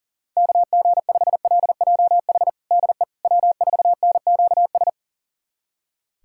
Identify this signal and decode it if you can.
Morse code, heard as a keyed tone, about 40 wpm, 700 Hz. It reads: KG5LJH DE W4NQS
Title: USA Callsign DE Callsign 40wpm